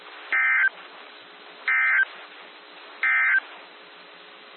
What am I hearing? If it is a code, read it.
Emergency Alert System (SAME) tones, heard as harsh buzzing bursts. It is heard from 0.2 to 3.5 s.
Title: NOAA Weather Radio - WXJ73